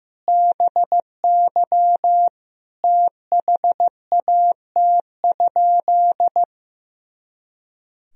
Morse code, heard as a keyed tone, about 15 wpm, 700 Hz. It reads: BY THAT?
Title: Sentences from Top 200 Words and Abbreviations 15wpm